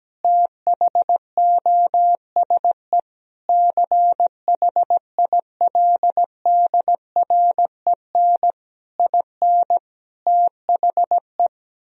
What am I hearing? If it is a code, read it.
Morse code, heard as a keyed tone, about 17 wpm, 700 Hz. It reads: THOSE CHILDREN IN THE
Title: Sentences from Top 1000 Words - Review Part 2 - 17wpm